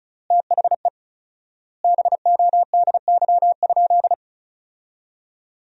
Morse code, heard as a keyed tone, about 34 wpm, 700 Hz. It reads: THE BODY?
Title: 4x Word-Spacing - Sentences from Top 400 Words - 35wpm